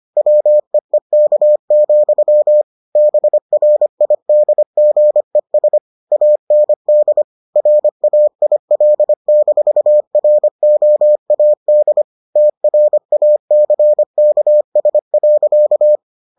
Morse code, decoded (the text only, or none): WEEK, BRIDGES AND RAIL-ROAD TRACKS.